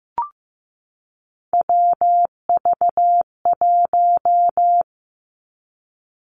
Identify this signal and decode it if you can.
Morse code, heard as a keyed tone, about 15 wpm, 700 Hz. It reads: WV1
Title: Three Challenging Letter-Number Combinations 15wpm